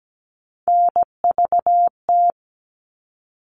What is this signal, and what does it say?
Morse code, heard as a keyed tone, about 17 wpm, 700 Hz. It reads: NVT
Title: All Cut Numbers - 3-Digit 17wpm